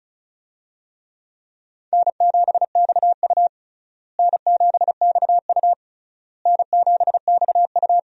Morse code, heard as a keyed tone, about 36 wpm, 700 Hz. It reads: N7XU N7XU N7XU
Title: USA Call Signs - 4 Characters - Repeated 3x - 35wpm